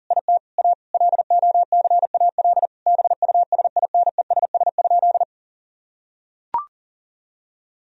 Morse code, heard as a keyed tone, about 40 wpm, 700 Hz. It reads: IT A LOCAL BUSINESS?
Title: Sentences from Top 1000 Words - Review Part 3 - 40wpm